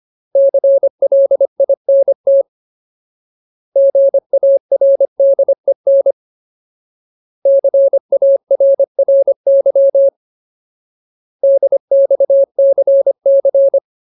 Morse code, heard as a keyed tone, about 25 wpm, 550 Hz. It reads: CLINT GARDEN CARRY DXCC
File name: Ultimate Word List / 25WPM Ultimate